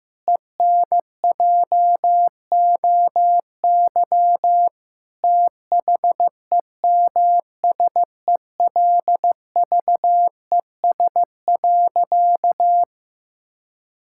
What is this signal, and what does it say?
Morse code, heard as a keyed tone, about 15 wpm, 700 Hz. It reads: ENJOY THEMSELVES.
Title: Sentences from Top 1000 Words - Review Part 5 - 15wpm